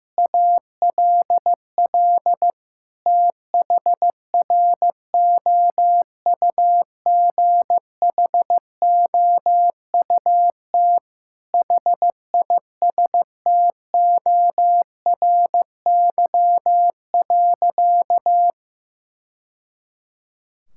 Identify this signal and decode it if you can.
Morse code, heard as a keyed tone, about 16 wpm, 700 Hz. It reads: ALL THROUGHOUT HISTORY.
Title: Sentences from Top 1000 Words - Review Part 5 - 15wpm